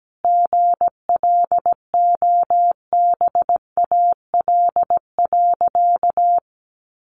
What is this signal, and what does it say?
Morse code, heard as a keyed tone, about 17 wpm, 700 Hz. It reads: GLOBAL.
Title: Sentences from Top 1200 Words 17wpm